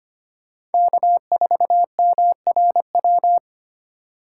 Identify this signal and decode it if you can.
Morse code, heard as a keyed tone, about 32 wpm, 700 Hz. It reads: K4MRW